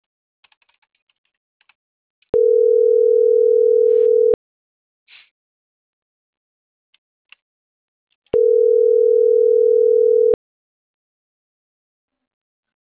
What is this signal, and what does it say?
Ringback tone (North America), 440 + 480 Hz.
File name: CENTRAL MAYAB / out vanessa.loyola - .2935